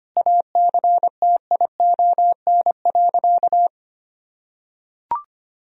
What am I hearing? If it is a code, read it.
Morse code, heard as a keyed tone, about 25 wpm, 700 Hz. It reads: ACTION.
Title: Sentences from Top 1000 Words - Review Part 4 - 25wpm